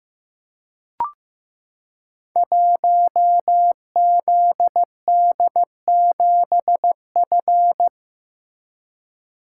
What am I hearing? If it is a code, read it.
Morse code, heard as a keyed tone, about 15 wpm, 700 Hz. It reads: 1ZD7F
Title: Code Groups - 5 Letter:Numbers - 15wpm